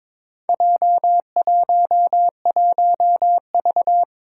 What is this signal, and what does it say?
Morse code, heard as a keyed tone, about 23 wpm, 700 Hz. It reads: J11V
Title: Four Challenging Letter-Number Combinations 22wpm